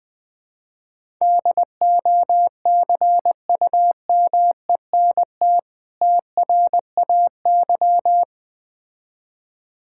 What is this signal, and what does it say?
Morse code, heard as a keyed tone, about 20 wpm, 690 Hz. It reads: DOCUMENT TRAY
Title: Difficult Words - Speed-Racing with Hints - 20wpm